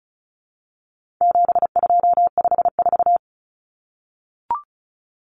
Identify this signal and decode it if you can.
Morse code, heard as a keyed tone, about 34 wpm, 700 Hz. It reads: 7254
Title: Numbers - 4-Digit 35wpm